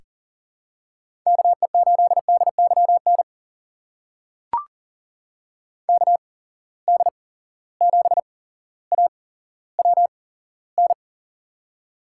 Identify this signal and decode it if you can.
Morse code, heard as a keyed tone, about 39 wpm, spaced as for 17 wpm, 700 Hz. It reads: KE8DYN KD7AWN
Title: USA Call Signs - 6 Characters - 8x Character Spacing - 40wpm